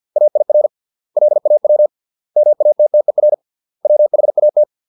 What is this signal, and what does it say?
Morse code, heard as a keyed tone, about 50 wpm, 600 Hz. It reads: AIR LAW MATTER WHAT